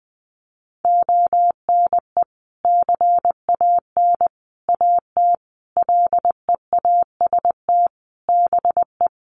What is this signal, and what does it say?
Morse code, heard as a keyed tone, about 20 wpm, 700 Hz. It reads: ONE CAN AT LEAST BE
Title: Sentences from Top 1200 Words 20wpm